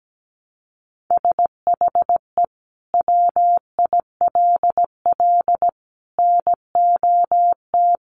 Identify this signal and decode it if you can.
Morse code, heard as a keyed tone, about 17 wpm, 700 Hz. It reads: SHE WILL NOT
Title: Sentences from Top 1100 Words 17wpm